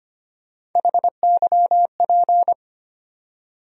Morse code, heard as a keyed tone, about 25 wpm, 700 Hz. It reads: HYP
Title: USA Call Sign Postfixes - 3 Letters 25wpm